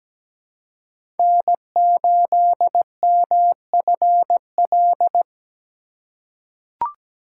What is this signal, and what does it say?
Morse code, heard as a keyed tone, about 17 wpm, 700 Hz. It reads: N8MFL
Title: USA Call Signs - 5 Characters - 3x Character Spacing - 17wpm